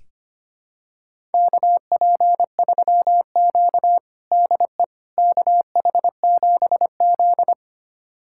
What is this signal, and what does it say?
Morse code, heard as a keyed tone, about 25 wpm, 700 Hz. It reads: KP3Q DE KH7Z
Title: USA Callsign DE Callsign - 4 Characters - 25wpm